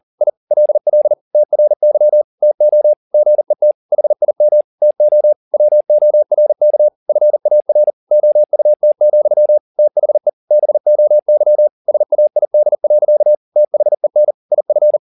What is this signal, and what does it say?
Morse code, heard as a keyed tone, about 40 wpm, 600 Hz. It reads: I LL TRY TO GET HIM TO WORK FAR OUT, THE BOY SAID. THEN IF